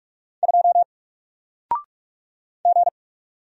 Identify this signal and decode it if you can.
Morse code, heard as a keyed tone, about 45 wpm, 700 Hz. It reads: 2 G
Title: Single Character (Letter:Number:Punctuation:Prosign) 45wpm